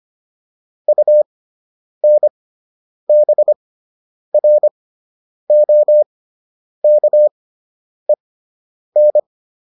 Morse code, heard as a keyed tone, about 25 wpm, spaced as for 9 wpm, 600 Hz. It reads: UNBROKEN